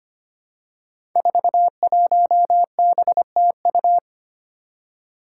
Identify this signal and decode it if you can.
Morse code, heard as a keyed tone, about 25 wpm, 700 Hz. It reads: 41BTU
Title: Five Challenging Letter-Number Combinations 25wpm